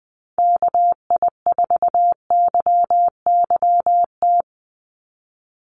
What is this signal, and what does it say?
Morse code, heard as a keyed tone, about 20 wpm, 700 Hz. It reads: KI4YYT
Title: USA Call Signs 20wpm